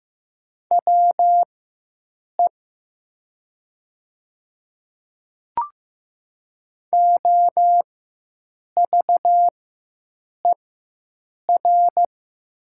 Morse code, heard as a keyed tone, about 15 wpm, spaced as for 7 wpm, 700 Hz. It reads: WE OVER
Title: 4x Character Spacing - Top 200 Words - 15wpm